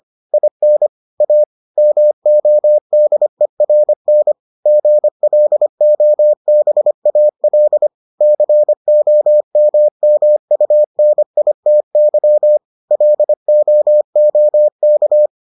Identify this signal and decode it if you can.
Morse code, heard as a keyed tone, about 25 wpm, 600 Hz. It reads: IN A MODERN GLOBAL COMMUNITY LOOK